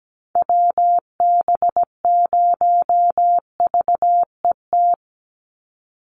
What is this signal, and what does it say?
Morse code, heard as a keyed tone, about 17 wpm, 700 Hz. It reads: WB0VET
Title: USA Call Signs 17wpm